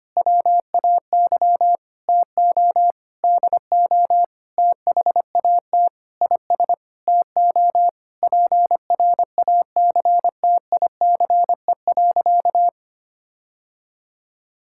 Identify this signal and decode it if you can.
Morse code, heard as a keyed tone, about 25 wpm, 700 Hz. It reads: WAY TO DO THAT IS TO PRACTICE.
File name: practice / 25wpm / Sentences with 9-Word Phrases - 25wpm